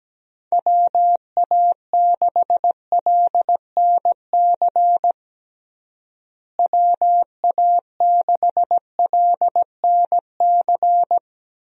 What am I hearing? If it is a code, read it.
Morse code, heard as a keyed tone, about 17 wpm, 700 Hz. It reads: WA6LNC WA6LNC
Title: USA Call Signs - 6 Characters - Repeated 3x - 17wpm